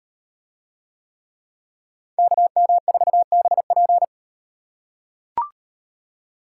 Morse code, heard as a keyed tone, about 38 wpm, 700 Hz. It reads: KM4BP